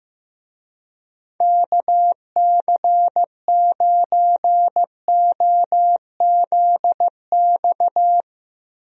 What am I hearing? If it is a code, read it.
Morse code, heard as a keyed tone, about 15 wpm, 700 Hz. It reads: KC9OZX
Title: USA Call Signs - 6 Characters - Repeated 2x - 15wpm